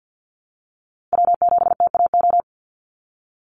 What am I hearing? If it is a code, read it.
Morse code, heard as a keyed tone, about 48 wpm, 700 Hz. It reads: W7TAO
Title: NA SOTA Chaser Callsigns 50wpm